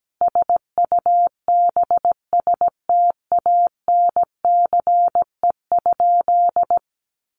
Morse code, heard as a keyed tone, about 17 wpm, 700 Hz. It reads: SUBSTANCE?
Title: Sentences from Top 1500 Words 17wpm